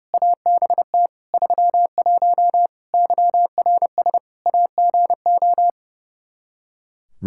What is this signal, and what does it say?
Morse code, heard as a keyed tone, about 32 wpm, 700 Hz. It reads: ABT 31 YRS AGO
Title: Practice QSO - Exchange 3 - More Info - 30wpm